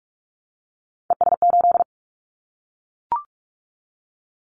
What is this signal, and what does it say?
Morse code, heard as a keyed tone, about 44 wpm, 700 Hz. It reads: ES8
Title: Callsign Prefixes for Top 100 DXCC - 45wpm